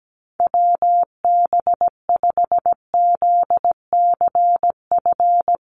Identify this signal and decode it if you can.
Morse code, heard as a keyed tone, about 17 wpm, 700 Hz. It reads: WB5ZCF
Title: USA Call Signs Repeated 3x 17wpm